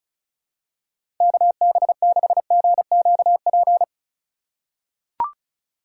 Morse code, heard as a keyed tone, about 36 wpm, 700 Hz. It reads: KB6GQP
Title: USA Call Signs - 6 Characters - 3x Character Spacing - 35wpm